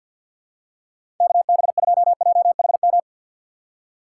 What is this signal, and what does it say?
Morse code, heard as a keyed tone, about 49 wpm, 690 Hz. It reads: KB2JHM
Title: USA Call Signs - 6 Characters - 3x Character Spacing - 50wpm